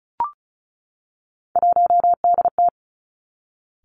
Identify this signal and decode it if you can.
Morse code, heard as a keyed tone, about 36 wpm, 700 Hz. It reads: 1DT
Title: Three Challenging Letter-Number Combinations 35wpm